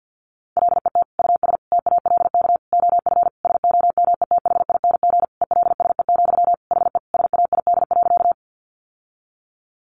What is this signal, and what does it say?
Morse code, heard as a keyed tone, about 50 wpm, 700 Hz. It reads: LET US TALK OF SOMETHING ELSE, HE SAID.
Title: Sentences from Top 800 Words 50wpm